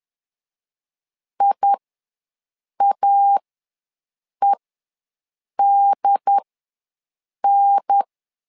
Morse code, heard as a keyed tone, about 10 wpm, spaced as for 6 wpm, 800 Hz. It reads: IAEDN
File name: LEZIONI / 3 MWRO / 3 MWRO ws 10wpm 2310